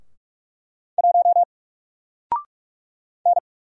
Morse code, heard as a keyed tone, about 46 wpm, 700 Hz. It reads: 1 N
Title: Single Letter-Number 45wpm